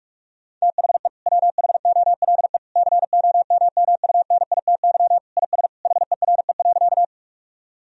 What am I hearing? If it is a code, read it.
Morse code, heard as a keyed tone, about 46 wpm, 700 Hz. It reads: THE WHOLE COMMUNITY IS HERE.
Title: Sentences from Top 1000 Words - Review Part 4 - 45wpm